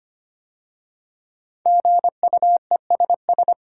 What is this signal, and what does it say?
Morse code, heard as a keyed tone, about 25 wpm, 700 Hz. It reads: GUESS
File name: 5 Letter Words - First 3 Letters Given - 25wpm